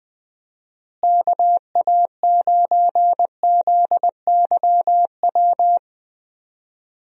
Morse code, heard as a keyed tone, about 20 wpm, 700 Hz. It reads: KA9ZYW
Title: USA Call Signs - 6 Characters - Repeated 3x - 20wpm